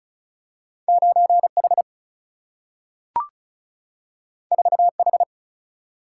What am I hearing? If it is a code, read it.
Morse code, heard as a keyed tone, about 36 wpm, 700 Hz. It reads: 9 H 4 H
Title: Single-Letter Single-Number 35wpm